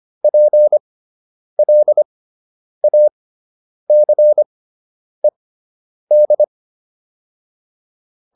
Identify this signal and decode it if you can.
Morse code, heard as a keyed tone, about 25 wpm, spaced as for 9 wpm, 600 Hz. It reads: PLACED